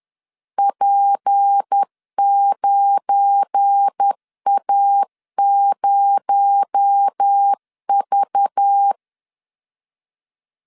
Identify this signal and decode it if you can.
Morse code, heard as a keyed tone, about 10 wpm, 800 Hz. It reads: P9A0V
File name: CW / DET / 1 9 1500C S C 10wpm 4557